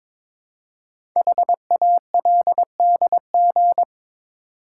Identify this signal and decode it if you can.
Morse code, heard as a keyed tone, about 22 wpm, 700 Hz. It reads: HALDG
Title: Code Groups - 5 Letters - 22wpm